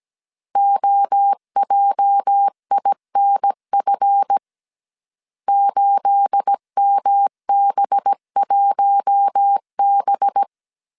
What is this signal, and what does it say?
Morse code, heard as a keyed tone, about 17 wpm, 800 Hz. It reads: OJINF 8MB1B
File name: CW / DET / 1 9 1500C S A 16wpm 2828